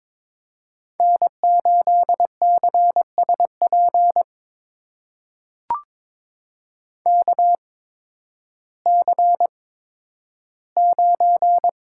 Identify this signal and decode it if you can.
Morse code, heard as a keyed tone, about 22 wpm, spaced as for 12 wpm, 700 Hz. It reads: N8CSP KC9
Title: USA Call Signs - 8x Character Spacing - 22wpm